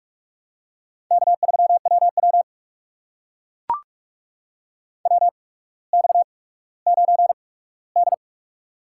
Morse code, heard as a keyed tone, about 44 wpm, spaced as for 19 wpm, 700 Hz. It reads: K3WW WX9D
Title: USA Call Signs - 4 Characters - 8x Character Spacing - 45wpm